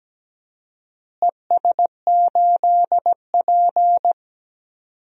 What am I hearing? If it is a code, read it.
Morse code, heard as a keyed tone, about 17 wpm, 700 Hz. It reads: ES8P
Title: DXCC-100 Call Signs - 4 Characters - Repeated 3x - 17wpm